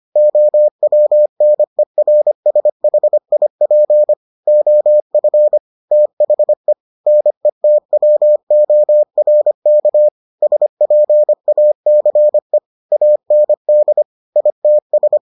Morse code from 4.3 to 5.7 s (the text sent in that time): OF